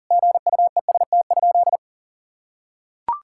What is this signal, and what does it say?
Morse code, heard as a keyed tone, about 40 wpm, 700 Hz. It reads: GUEST?
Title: Sentences from Top 1600 Words 40wpm